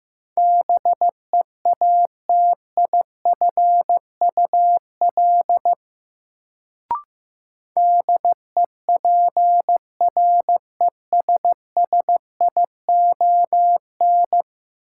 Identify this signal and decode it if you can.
Morse code, heard as a keyed tone, about 15 wpm, 700 Hz. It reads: BEATIFUL DEPRESSION
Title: Sentences from Top 1300 Words - New Word Drill Alt - 15wpm